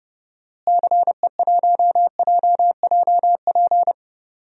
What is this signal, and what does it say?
Morse code, heard as a keyed tone, about 32 wpm, 700 Hz. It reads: CE1JJP
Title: DXCC-100 Call Signs - 6 Characters - Repeated 2x - 30wpm